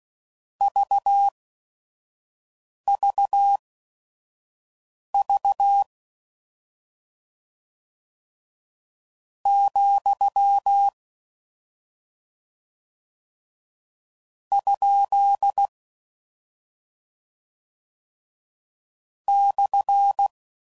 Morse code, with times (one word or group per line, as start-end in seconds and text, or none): 0.6-5.8 VVV
9.5-10.9 ,
14.5-15.7 ?
19.3-20.3 /